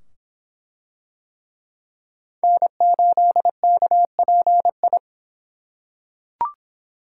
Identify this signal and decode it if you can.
Morse code, heard as a keyed tone, about 24 wpm, 700 Hz. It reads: N8KPI